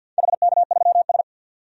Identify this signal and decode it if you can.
Morse code, heard as a keyed tone, about 52 wpm, 700 Hz. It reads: HK3S
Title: DXCC-100 Call Signs - 4 Characters - 50wpm